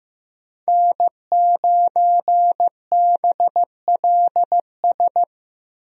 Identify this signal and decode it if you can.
Morse code, heard as a keyed tone, about 15 wpm, 700 Hz. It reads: N9BLS
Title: USA Call Signs - 5 Characters - Repeated 2x - 15wpm